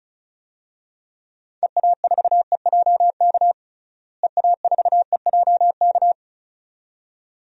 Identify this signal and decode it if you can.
Morse code, heard as a keyed tone, about 36 wpm, 700 Hz. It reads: EA4EJK EA4EJK
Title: DXCC-100 Call Signs - 6 Characters - Repeated 2x - 35wpm